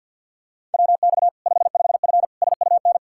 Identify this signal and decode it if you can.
Morse code, heard as a keyed tone, about 50 wpm, 700 Hz. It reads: WX 55F SUN
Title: Practice QSO - Exchange 2 - Info - 50wpm